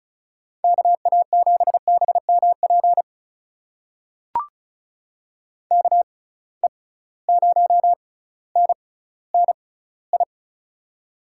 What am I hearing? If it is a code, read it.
Morse code, heard as a keyed tone, about 36 wpm, spaced as for 18 wpm, 700 Hz. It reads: KA7BMP KE0NNI
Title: USA Call Signs - 6 Characters - 6x Character Spacing - 35wpm